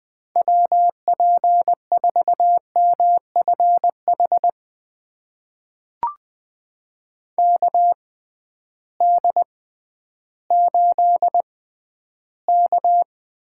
Morse code, heard as a keyed tone, about 20 wpm, spaced as for 12 wpm, 700 Hz. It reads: WP4MFH KD8K
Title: USA Call Signs - 6 Characters - 6x Character Spacing - 20wpm